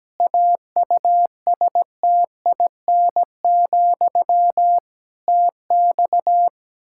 Morse code, heard as a keyed tone, about 17 wpm, 700 Hz. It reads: AUSTIN, TX
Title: Practice QSO - Exchange 1 - Beginning - 17wpm